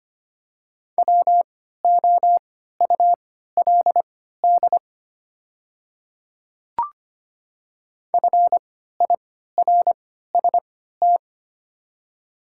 Morse code, heard as a keyed tone, about 25 wpm, spaced as for 14 wpm, 690 Hz. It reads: WOULD FIRST
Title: 3x Character Spacing - Top 200 Words - 25wpm